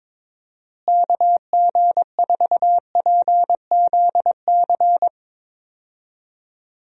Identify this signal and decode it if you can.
Morse code, heard as a keyed tone, about 22 wpm, 700 Hz. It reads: KG4PZC
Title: USA Call Signs - 6 Characters - Repeated 3x - 22wpm